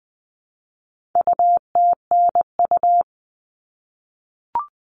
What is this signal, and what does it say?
Morse code, heard as a keyed tone, about 20 wpm, 700 Hz. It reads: UTNU
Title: All Cut Numbers - 4-Digit 20wpm